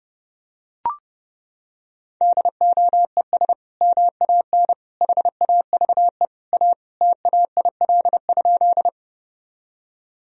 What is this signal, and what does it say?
Morse code, heard as a keyed tone, about 30 wpm, 700 Hz. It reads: DOES MAN HAVE A TAIL?
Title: Sentences from Top 1000 Words - Review Part 3 - 30wpm